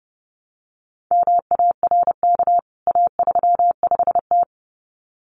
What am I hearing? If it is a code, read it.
Morse code, heard as a keyed tone, about 30 wpm, 700 Hz. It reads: MARK A35T
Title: CWT Exchange 30wpm